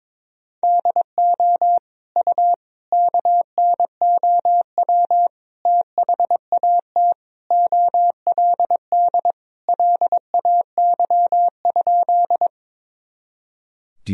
Morse code, heard as a keyed tone, about 21 wpm, 700 Hz. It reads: DO U KNOW THAT OLD LAY?
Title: Sentences from Top 400 Words and Abbreviations 22wpm